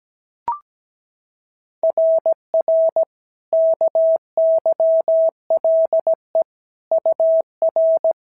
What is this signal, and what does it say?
Morse code, heard as a keyed tone, about 16 wpm, 650 Hz. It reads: RR KYLE UR
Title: SOTA QSO Practice 17wpm